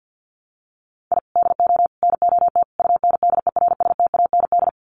Morse code, heard as a keyed tone, about 50 wpm, 700 Hz. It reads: I DO NOT UNDERSTAND
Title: Sets of 4 Words 50wpm